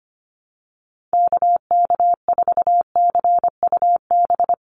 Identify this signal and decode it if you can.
Morse code, heard as a keyed tone, about 25 wpm, 700 Hz. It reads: KK4CUB
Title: USA Call Signs 25wpm